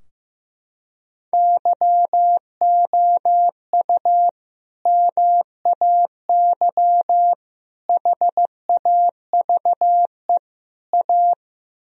Morse code, heard as a keyed tone, about 15 wpm, 700 Hz. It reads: YOU MAY HAVE A
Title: Sentences from Top 1000 Words - Review Part 2 - 15wpm